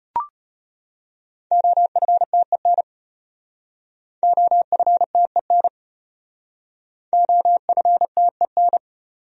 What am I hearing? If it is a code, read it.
Morse code, heard as a keyed tone, about 33 wpm, 700 Hz. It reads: OFTEN OFTEN OFTEN